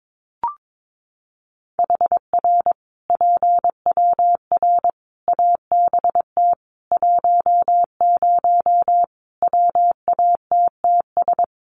Morse code, heard as a keyed tone, about 22 wpm, 700 Hz. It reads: HR PWR ABT 10 WATTS
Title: Common QSO Phrases 22wpm